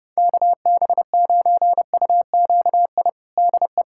Morse code, heard as a keyed tone, about 30 wpm, 700 Hz. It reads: KB9UQI DE
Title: USA Callsign DE Callsign - 6 Characters - 30wpm